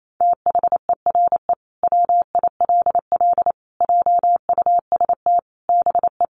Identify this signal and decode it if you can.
Morse code, heard as a keyed tone, about 28 wpm, 700 Hz. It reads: THERE WILL JUST BE
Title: Sentences from Top 200 Words 28wpm